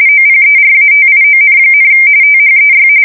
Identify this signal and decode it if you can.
RTTY, about 45.45 baud. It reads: OPERATION, THEY
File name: fsk170 / USS-Massachusetts